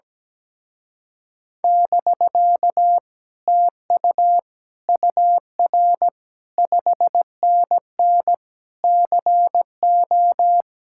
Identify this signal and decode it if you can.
Morse code, heard as a keyed tone, about 17 wpm, 700 Hz. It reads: <BK> TU UR 5NN CO
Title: Speed-Racing Common QSO Phrases - 17wpm